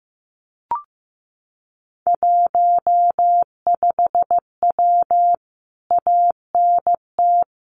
Morse code, heard as a keyed tone, about 15 wpm, 700 Hz. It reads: 15W ANT
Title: Common QSO Phrases Encore 15wpm